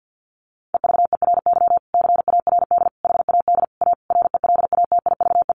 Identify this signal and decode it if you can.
Morse code, heard as a keyed tone, about 50 wpm, 700 Hz. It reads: EVERY CARD HAD A RELATIVE
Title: Sentences from Top 1400 Words 50wpm